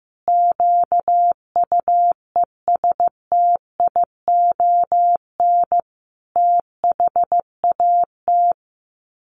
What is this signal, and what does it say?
Morse code, heard as a keyed tone, about 15 wpm, 700 Hz. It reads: QUESTION THAT
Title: Sets of 5 Words Encore 15wpm